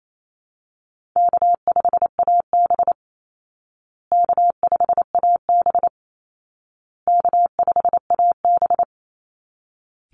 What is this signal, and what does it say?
Morse code, heard as a keyed tone, about 28 wpm, 700 Hz. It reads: K5AB K5AB K5AB
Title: Easy Call Signs Repeated 3x - 28wpm